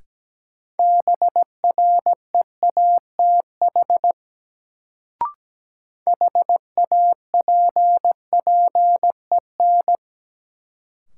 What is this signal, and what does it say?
Morse code, heard as a keyed tone, about 17 wpm, 700 Hz. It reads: BREATH HAPPEN
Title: Sets of 5 Words Encore - New Word Drill - 17wpm